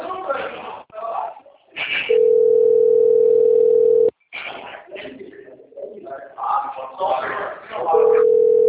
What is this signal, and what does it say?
Ringback tone (North America), 440 + 480 Hz, with speech.
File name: rg-1600-unknown- - - .11556